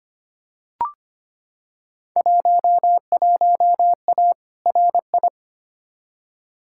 Morse code, heard as a keyed tone, about 25 wpm, 700 Hz. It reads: 11A RI
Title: Field Day Exchanges Encore - 25wpm